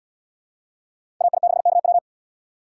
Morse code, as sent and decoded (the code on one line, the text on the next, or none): .- .. -.... -.- .--
AI6KW